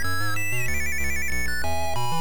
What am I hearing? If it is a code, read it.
There is no code.